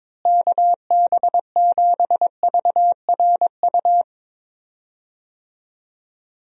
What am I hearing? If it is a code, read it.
Morse code, heard as a keyed tone, about 21 wpm, 700 Hz. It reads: KB7VRU
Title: USA Call Signs - 6 Characters - Repeated 2x - 22wpm